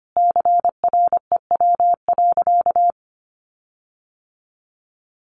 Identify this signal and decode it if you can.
Morse code, heard as a keyed tone, about 25 wpm, 700 Hz. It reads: CREW.
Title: Sentences from Top 1500 Words 25wpm